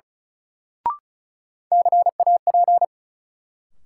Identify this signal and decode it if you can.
Morse code, heard as a keyed tone, about 36 wpm, 700 Hz. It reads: CAP
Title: Sentences from Top 1600 Words - New Word Drill Alt - 35wpm